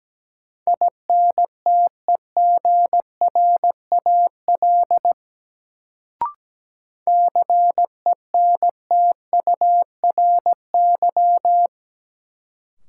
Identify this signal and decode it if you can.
Morse code, heard as a keyed tone, about 17 wpm, 700 Hz. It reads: INTEGRAL CENTURY
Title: Sets of 5 Words - New Word Drill - 17wpm